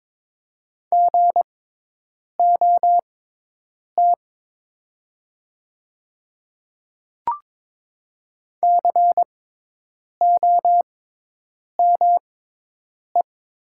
Morse code, heard as a keyed tone, about 22 wpm, spaced as for 8 wpm, 700 Hz. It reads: GOT COME
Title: 6x Character Spacing - Top 300 Words - 22wpm